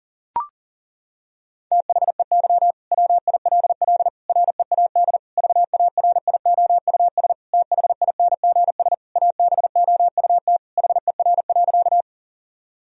Morse code, heard as a keyed tone, about 40 wpm, 700 Hz. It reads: THEY WILL READ VARIOUS THINGS ABOUT HER.
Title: Sentences from Top 1000 Words - Review Part 4 - 40wpm